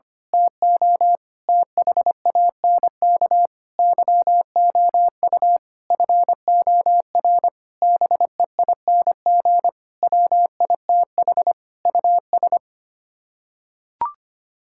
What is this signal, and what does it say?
Morse code, heard as a keyed tone, about 25 wpm, 700 Hz. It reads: TO THANK YOU FOR BEING WITH US
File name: Sets of 9 Words - 25wpm